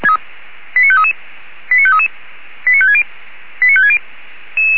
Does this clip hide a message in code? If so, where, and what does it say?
Yes: five-tone selective call (ZVEI) from 0.8 to 4.0 s, heard as quick runs of five tones, tones of about 80 ms. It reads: 89533 89533 89560 89560